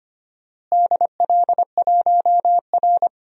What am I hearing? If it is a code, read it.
Morse code, heard as a keyed tone, about 25 wpm, 700 Hz. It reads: DL1R
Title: DXCC-100 Call Signs - 4 Characters - Repeated 3x - 25wpm